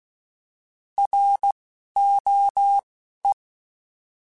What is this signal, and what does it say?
Morse code, heard as a keyed tone, about 16 wpm, 800 Hz. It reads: R O E